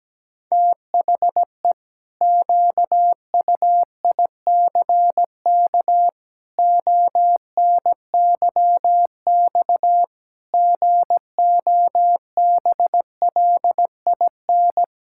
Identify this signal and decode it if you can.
Morse code, heard as a keyed tone, about 17 wpm, 700 Hz. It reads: THE QUICK ONYX GOBLIN
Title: Pangram Challenge v2 - 17wpm